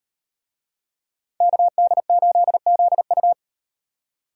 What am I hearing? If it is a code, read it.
Morse code, heard as a keyed tone, about 38 wpm, 700 Hz. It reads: KD8ZU